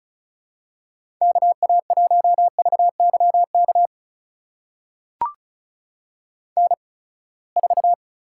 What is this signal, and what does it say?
Morse code, heard as a keyed tone, about 36 wpm, spaced as for 24 wpm, 700 Hz. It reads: KA1VYK N4
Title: USA Call Signs - 8x Character Spacing - 35wpm